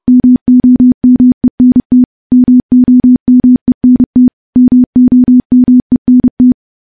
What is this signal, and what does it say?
Morse code, heard as a keyed tone, about 32 wpm, 250 Hz. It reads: MOMENT MOMENT MOMENT
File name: morse-training-30WPM